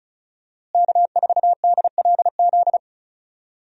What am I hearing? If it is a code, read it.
Morse code, heard as a keyed tone, about 36 wpm, 700 Hz. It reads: K4DLZ
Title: USA Call Signs - 5 Characters - Repeated 3x - 35wpm